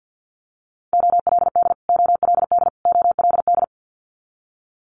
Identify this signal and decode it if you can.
Morse code, heard as a keyed tone, about 50 wpm, 690 Hz. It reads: OLD OLD OLD
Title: Multi-letter Phonemes Repeated 3x 50wpm